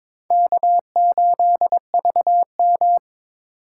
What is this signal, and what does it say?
Morse code, heard as a keyed tone, about 21 wpm, 700 Hz. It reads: K8VM
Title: USA Call Signs - 4 Characters - 2x Character Spacing - 22wpm